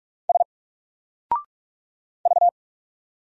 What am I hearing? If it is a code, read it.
Morse code, heard as a keyed tone, about 42 wpm, 700 Hz. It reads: S V
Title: Single Character (Letter:Number:Punctuation:Prosign) Alt 45wpm